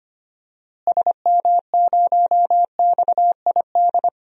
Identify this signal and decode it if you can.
Morse code, heard as a keyed tone, about 25 wpm, 700 Hz. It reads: SM0XID
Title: DXCC-100 Call Signs - 6 Characters - Repeated 2x - 25wpm